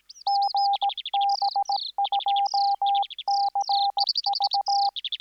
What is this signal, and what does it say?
Morse code, heard as a keyed tone, about 17 wpm, 800 Hz. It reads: GB2CV